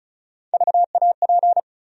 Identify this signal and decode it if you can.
Morse code, heard as a keyed tone, about 36 wpm, 700 Hz. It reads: VAP
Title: USA Call Sign Postfixes - 3 Letters 35wpm